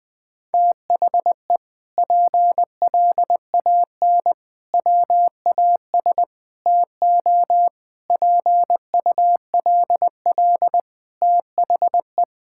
Morse code, heard as a keyed tone, about 20 wpm, 700 Hz. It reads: THE PLAN WAS TO PULL THE
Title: Sentences with 8-Word Phrases - 20wpm